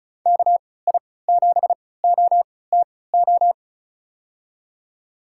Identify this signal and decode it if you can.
Morse code, heard as a keyed tone, about 34 wpm, spaced as for 20 wpm, 700 Hz. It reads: KI7OTO
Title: USA Call Signs - 6 Characters - 3x Character Spacing - 35wpm